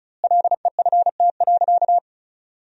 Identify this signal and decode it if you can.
Morse code, heard as a keyed tone, about 36 wpm, 700 Hz. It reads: LEFT.
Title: 2x Word-Spacing - Sentences from Top 300 Words - 35wpm